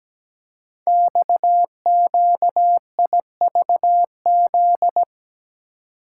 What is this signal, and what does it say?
Morse code, heard as a keyed tone, about 17 wpm, 700 Hz. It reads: XQIVZ
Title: Code Groups - 5 Letters - 17wpm